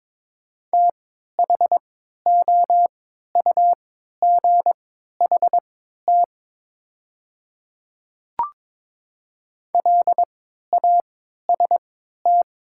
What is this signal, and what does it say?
Morse code, heard as a keyed tone, about 22 wpm, spaced as for 12 wpm, 700 Hz. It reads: THOUGHT LAST
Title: 3x Character Spacing - Top 300 Words - 22wpm